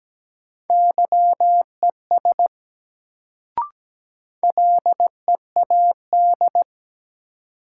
Morse code, heard as a keyed tone, about 17 wpm, 700 Hz. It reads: YES LEAD
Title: Sentences from Top 500 Words - New Word Drill - 17wpm